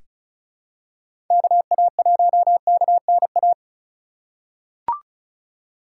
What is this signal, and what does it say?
Morse code, heard as a keyed tone, about 36 wpm, 700 Hz. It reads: KA1KNA
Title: USA Call Signs - 6 Characters - 3x Character Spacing - 35wpm